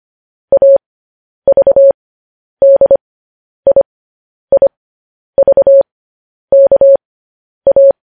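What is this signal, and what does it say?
Morse code, heard as a keyed tone, about 25 wpm, spaced as for 10 wpm, 550 Hz. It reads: AVDIIVKA
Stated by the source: Morse code news headlines.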